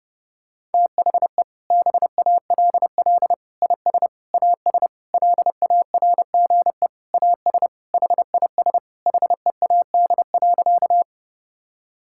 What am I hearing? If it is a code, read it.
Morse code, heard as a keyed tone, about 30 wpm, 700 Hz. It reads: THE BALL IS AS LARGE AS HIS HEAD.
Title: Sentences from Top 1000 Words - Review Part 6 - 30wpm